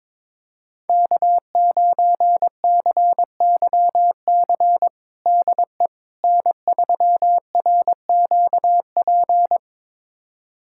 Morse code, heard as a keyed tone, about 22 wpm, 700 Hz. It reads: K9CYC DE N3RQP
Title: USA Callsign DE Callsign - 5 Characters - 22wpm